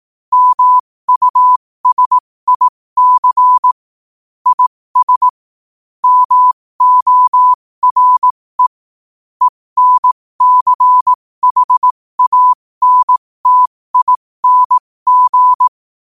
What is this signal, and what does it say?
Morse code, heard as a keyed tone, about 18 wpm, spaced as for 15 wpm, 1000 Hz. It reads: MUSIC IS MORE ENCHANTING
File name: qotdfiles / 15wpm